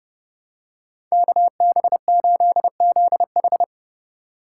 Morse code, heard as a keyed tone, about 29 wpm, 700 Hz. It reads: KB8ZH